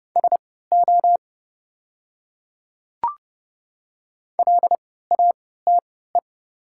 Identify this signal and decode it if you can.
Morse code, heard as a keyed tone, about 29 wpm, spaced as for 17 wpm, 690 Hz. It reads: SO LATE
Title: 3x Character Spacing - Top 300 Words - 30wpm